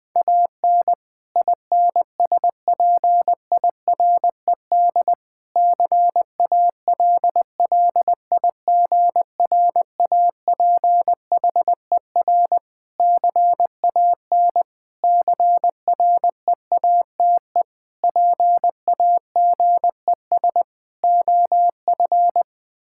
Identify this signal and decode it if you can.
Morse code, heard as a keyed tone, about 20 wpm, 700 Hz. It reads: AN INSPIRED CALLIGRAPHER CAN CREATE PAGES OF
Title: Pangram Challenge v2 - 20wpm